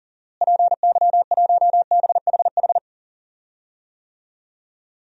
Morse code, heard as a keyed tone, about 40 wpm, 700 Hz. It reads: PY1BHH
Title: DXCC-100 Call Signs - 6 Characters - Repeated 2x - 40wpm